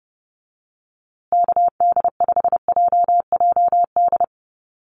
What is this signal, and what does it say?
Morse code, heard as a keyed tone, about 30 wpm, 700 Hz. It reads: KD5JJD
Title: USA Call Signs Repeated 3x 30wpm